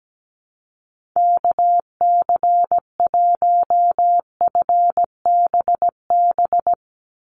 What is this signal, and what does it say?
Morse code, heard as a keyed tone, about 17 wpm, 700 Hz. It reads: KC1FBB
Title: USA Call Signs 17wpm